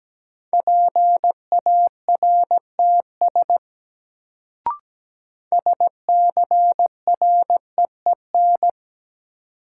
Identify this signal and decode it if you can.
Morse code, heard as a keyed tone, about 17 wpm, 700 Hz. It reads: PARTS SCREEN
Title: Sentences from Top 1000 Words - Review Part 6 - New Word Drill - 17wpm